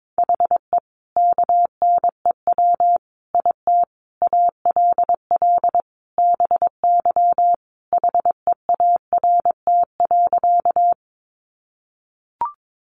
Morse code, heard as a keyed tone, about 22 wpm, 700 Hz. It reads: HE KNEW IT ALL BY HEART.
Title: Sentences from Top 1200 Words 22wpm